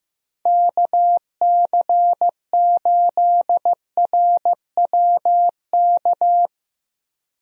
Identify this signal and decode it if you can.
Morse code, heard as a keyed tone, about 15 wpm, 700 Hz. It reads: KC8RWK
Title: USA Call Signs - 6 Characters - Repeated 2x - 15wpm